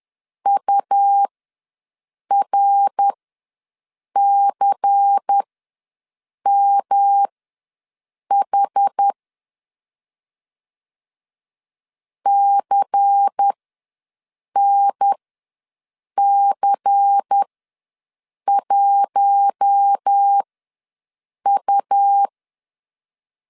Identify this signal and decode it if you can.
Morse code, heard as a keyed tone, about 10 wpm, spaced as for 6 wpm, 800 Hz. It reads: URCMH CNC1U
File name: LEZIONI / 5 LBVC / 5 LBVC ws 10wpm 3733